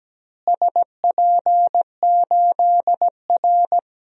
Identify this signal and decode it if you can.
Morse code, heard as a keyed tone, about 17 wpm, 700 Hz. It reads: SP8R
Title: DXCC-100 Call Signs - 4 Characters - Repeated 3x - 17wpm